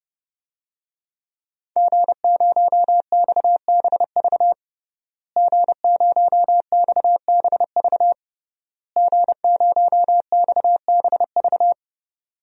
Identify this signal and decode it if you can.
Morse code, heard as a keyed tone, about 30 wpm, 700 Hz. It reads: G0XBV G0XBV G0XBV
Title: DXCC-100 Call Signs - 5 Characters - Repeated 3x - 30wpm